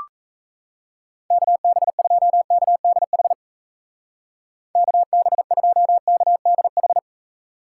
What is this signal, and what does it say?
Morse code, heard as a keyed tone, about 36 wpm, 700 Hz. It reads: KB2KDH KB2KDH